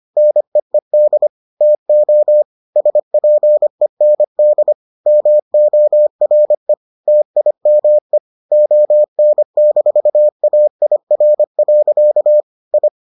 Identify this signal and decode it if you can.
Morse code, heard as a keyed tone, about 25 wpm, 600 Hz. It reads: NEED TO SPEND MORE TIME ON-AIR. I